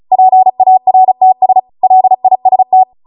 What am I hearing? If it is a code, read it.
Morse code, heard as a keyed tone, about 36 wpm, 750 Hz. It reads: PARTS LIST